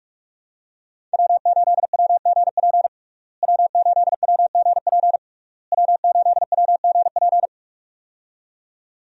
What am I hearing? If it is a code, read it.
Morse code, heard as a keyed tone, about 46 wpm, 700 Hz. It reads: W8WGP W8WGP W8WGP
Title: USA Call Signs - 5 Characters - Repeated 3x - 45wpm